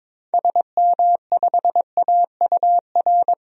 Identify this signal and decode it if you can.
Morse code, heard as a keyed tone, about 22 wpm, 700 Hz. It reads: SM5AUR
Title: DXCC-100 Call Signs - 6 Characters - 22wpm